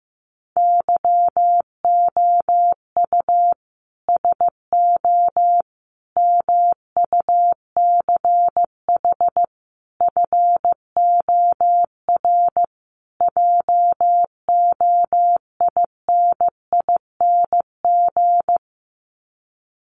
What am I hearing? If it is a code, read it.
Morse code, heard as a keyed tone, about 14 wpm, 700 Hz. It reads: YOU SO MUCH FOR JOINING
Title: Sets of 5 Words Encore2 15wpm